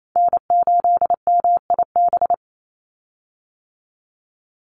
Morse code, heard as a keyed tone, about 28 wpm, 700 Hz. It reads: N8MIB
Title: USA Call Signs 28wpm